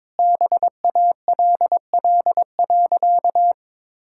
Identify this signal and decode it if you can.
Morse code, heard as a keyed tone, about 22 wpm, 690 Hz. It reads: BALL.
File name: practice / 22wpm / Easy Sentences with Hints - 22wpm